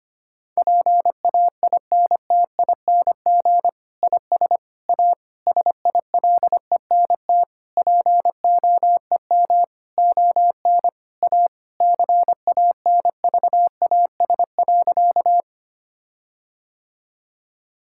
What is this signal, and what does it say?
Morse code, heard as a keyed tone, about 25 wpm, 700 Hz. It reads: PAINTING IS A SILENT POEM ON A CANVAS.
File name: practice / 25wpm / Difficult Sentences with Hints - 25wpm